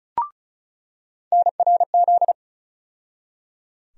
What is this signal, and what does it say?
Morse code, heard as a keyed tone, about 34 wpm, 700 Hz. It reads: NRZ
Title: USA Call Sign Postfixes - 3 Letters 35wpm